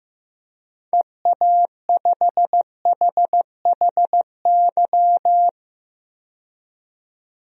Morse code, heard as a keyed tone, about 15 wpm, 700 Hz. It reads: EA5HHY
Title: DXCC-100 Call Signs - 6 Characters - 15wpm